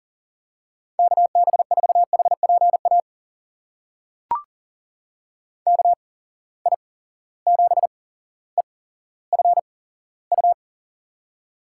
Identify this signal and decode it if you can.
Morse code, heard as a keyed tone, about 40 wpm, spaced as for 17 wpm, 700 Hz. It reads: KB4HPA KI7EFU
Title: USA Call Signs - 6 Characters - 8x Character Spacing - 40wpm